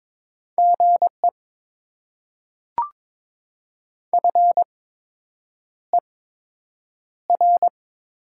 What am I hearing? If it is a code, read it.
Morse code, heard as a keyed tone, about 22 wpm, spaced as for 8 wpm, 700 Hz. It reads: GE FER
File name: Top 20 QSO Elements - 8x Character Spacing - 22wpm